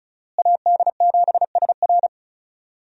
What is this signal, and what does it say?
Morse code, heard as a keyed tone, about 34 wpm, 700 Hz. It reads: AD7SR
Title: USA Call Signs - 5 Characters - Repeated 3x - 35wpm